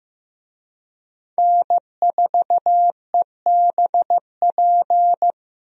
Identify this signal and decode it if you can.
Morse code, heard as a keyed tone, about 16 wpm, 700 Hz. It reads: N4EBP
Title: USA Call Signs - 5 Characters - 15wpm